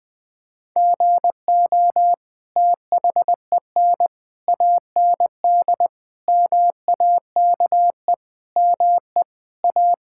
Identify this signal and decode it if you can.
Morse code, heard as a keyed tone, about 21 wpm, 700 Hz. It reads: GO THEN AND MAKE ME A
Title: Sentences from Top 1000 Words - Review Part 5 - 20wpm